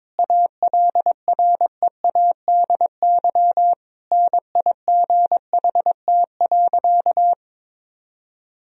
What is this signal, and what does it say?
Morse code, heard as a keyed tone, about 22 wpm, 700 Hz. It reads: ALREADY NIGHT.
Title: Sentences from Top 1000 Words - Review Part 1 - 22wpm